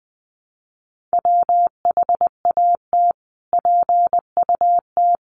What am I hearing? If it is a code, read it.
Morse code, heard as a keyed tone, about 20 wpm, 700 Hz. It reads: WHAT PUT
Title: Sentences from Top 600 Words 20wpm